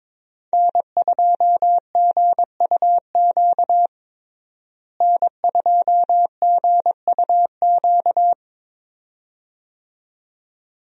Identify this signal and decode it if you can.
Morse code, heard as a keyed tone, about 22 wpm, 700 Hz. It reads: N2GUQ N2GUQ
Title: USA Call Signs - 5 Characters - Repeated 2x - 22wpm